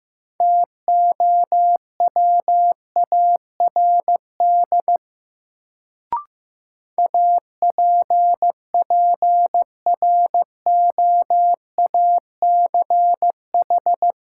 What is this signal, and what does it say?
Morse code, heard as a keyed tone, about 15 wpm, 700 Hz. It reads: TOWARD APPROACH
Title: Sentences from Top 1000 Words - Review Part 6 - New Word Drill - 15wpm